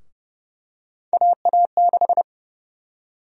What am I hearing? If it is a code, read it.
Morse code, heard as a keyed tone, about 30 wpm, 700 Hz. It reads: AA6
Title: USA Call Sign Prefixes - 30wpm